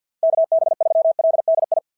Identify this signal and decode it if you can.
Morse code, heard as a keyed tone, about 53 wpm, 690 Hz. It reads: KB3LDI
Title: SOTA QSO Practice 50wpm